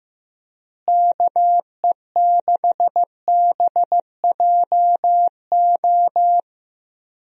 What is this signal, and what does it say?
Morse code, heard as a keyed tone, about 14 wpm, 700 Hz. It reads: KE6BJO
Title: USA Call Signs - 6 Characters - Repeated 3x - 15wpm